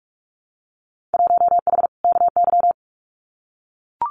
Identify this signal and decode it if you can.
Morse code, heard as a keyed tone, about 44 wpm, 700 Hz. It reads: 1H KY
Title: Winter Field Day Exchanges 45wpm